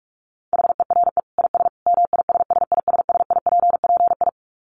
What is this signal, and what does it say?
Morse code, heard as a keyed tone, about 44 wpm, 700 Hz. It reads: HERE IS MISSISSIPPI
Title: Dit Challenge 45wpm